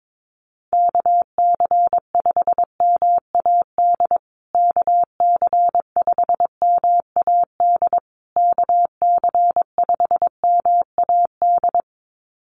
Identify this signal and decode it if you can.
Morse code, heard as a keyed tone, about 22 wpm, 700 Hz. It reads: KC5MAD KC5MAD KC5MAD
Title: USA Call Signs Repeated 3x 22wpm